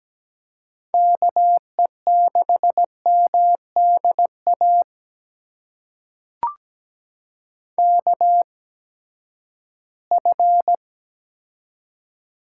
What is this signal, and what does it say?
Morse code, heard as a keyed tone, about 17 wpm, spaced as for 12 wpm, 700 Hz. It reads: KE6MDA KF
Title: USA Call Signs - 6 Characters - 8x Character Spacing - 17wpm